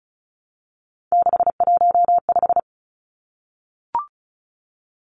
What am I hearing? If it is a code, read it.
Morse code, heard as a keyed tone, about 34 wpm, 700 Hz. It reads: 615
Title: Numbers - 3-Digit 35wpm